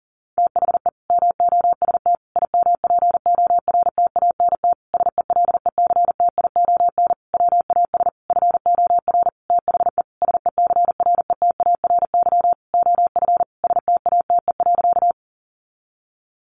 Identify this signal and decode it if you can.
Morse code, heard as a keyed tone, about 40 wpm, 700 Hz. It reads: THE MOST IMPORTANT SELECTION WAS FOR THE SECRETARY OF STATE.
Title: Sentences from Top 1400 Words 40wpm